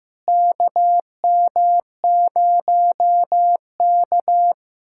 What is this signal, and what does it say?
Morse code, heard as a keyed tone, about 16 wpm, 700 Hz. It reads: KM0K
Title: USA Callsign DE Callsign - 4 Characters - 15wpm